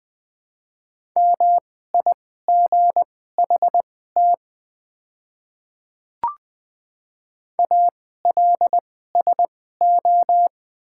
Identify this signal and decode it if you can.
Morse code, heard as a keyed tone, about 21 wpm, spaced as for 15 wpm, 700 Hz. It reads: MIGHT ALSO
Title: 2x Character Spacing - Top 300 Words - 20wpm